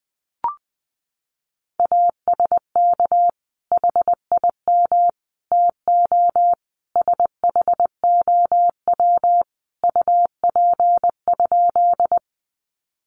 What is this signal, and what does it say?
Morse code, heard as a keyed tone, about 20 wpm, 700 Hz. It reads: ASK HIM TO SHOW UP?
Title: Sentences from Top 200 Words 20wpm